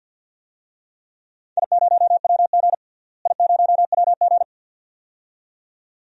Morse code, heard as a keyed tone, about 53 wpm, 700 Hz. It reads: I0WG I0WG
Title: DXCC-100 Call Signs - 4 Characters - Repeated 2x - 50wpm